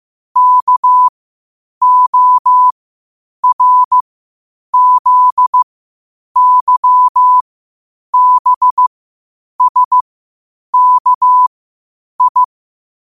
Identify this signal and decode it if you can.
Morse code, heard as a keyed tone, about 15 wpm, spaced as for 8 wpm, 1000 Hz. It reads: KORZYBSKI
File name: qotdfiles / 8wpm